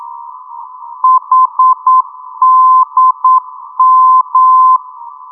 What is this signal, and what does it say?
Morse code, heard as a keyed tone, about 9 wpm, 1030 Hz. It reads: HDM
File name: europe / d-8023 - ndb coleman hdm-334 (2004) (rainer brannolte, d) xc